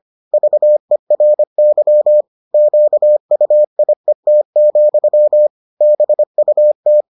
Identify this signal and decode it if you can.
Morse code, heard as a keyed tone, about 25 wpm, 600 Hz. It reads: VERY QUIET, BUT